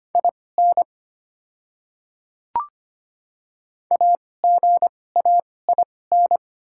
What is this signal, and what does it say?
Morse code, heard as a keyed tone, about 25 wpm, spaced as for 18 wpm, 690 Hz. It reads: IN AGAIN
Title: 2x Character Spacing - Top 200 Words - 25wpm